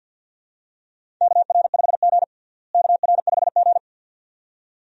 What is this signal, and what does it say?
Morse code, heard as a keyed tone, about 53 wpm, 700 Hz. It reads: KR5G KR5G
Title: USA Call Signs - 4 Characters - Repeated 2x - 50wpm